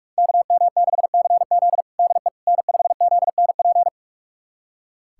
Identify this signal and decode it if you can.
Morse code, heard as a keyed tone, about 46 wpm, 700 Hz. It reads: KM6CZ DE N5ZNP
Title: USA Callsign DE Callsign - 5 Characters - 45wpm